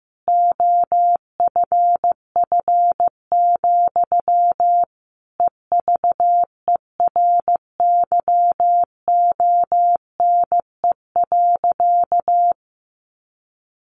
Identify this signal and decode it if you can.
Morse code, heard as a keyed tone, about 14 wpm, 700 Hz. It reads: OFF, EVERYONE.
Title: A Sound Of Thunder 15wpm